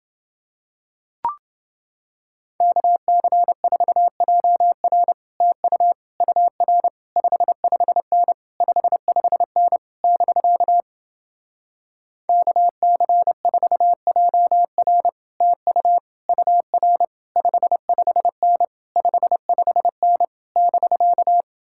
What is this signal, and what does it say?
Morse code, heard as a keyed tone, about 30 wpm, 700 Hz. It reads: KC4JR TU UR 55N 55N <BK> KC4JR TU UR 55N 55N <BK>